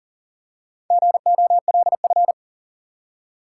Morse code, heard as a keyed tone, about 40 wpm, 700 Hz. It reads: GOLF
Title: Four Letter Words - 40wpm